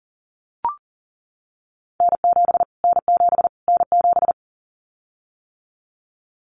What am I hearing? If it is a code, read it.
Morse code, heard as a keyed tone, about 38 wpm, 700 Hz. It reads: N7 N7 N7
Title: Callsign Prefixes for Top 25 DXCC - Repeated 3x - 40wpm